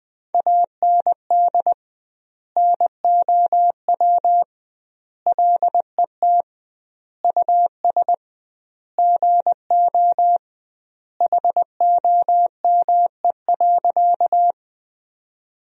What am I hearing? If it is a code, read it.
Morse code, heard as a keyed tone, about 20 wpm, 700 Hz. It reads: AND NOW LET US GO HOME.
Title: 2x Word-Spacing - Sentences from Top 300 Words - 20wpm